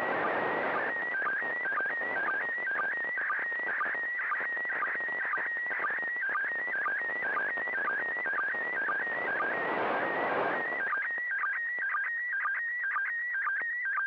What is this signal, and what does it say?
SSTV (slow-scan television), PD 120.
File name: SSTV ISS